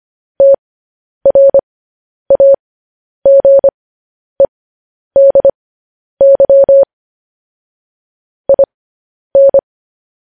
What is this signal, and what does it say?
Morse code, heard as a keyed tone, about 25 wpm, spaced as for 10 wpm, 550 Hz. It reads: TRAGEDY IN
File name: podcast / News headlines-1x-25x10